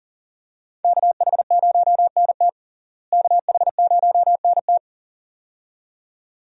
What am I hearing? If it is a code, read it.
Morse code, heard as a keyed tone, about 40 wpm, 700 Hz. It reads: KH0NT KH0NT
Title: USA Call Signs - 5 Characters - Repeated 2x - 40wpm